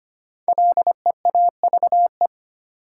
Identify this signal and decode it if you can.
Morse code, heard as a keyed tone, about 25 wpm, 700 Hz. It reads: LEAVE
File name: 5 Letter Words - First 3 Letters Given - 25wpm